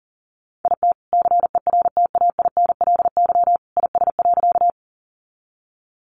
Morse code, heard as a keyed tone, about 40 wpm, 700 Hz. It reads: IT CERTAINLY IS.
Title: Sentences from Top 900 Words 40wpm